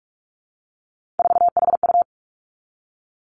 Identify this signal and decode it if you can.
Morse code, heard as a keyed tone, about 46 wpm, 700 Hz. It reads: 4HU
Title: Three Challenging Letter-Number Combinations 45wpm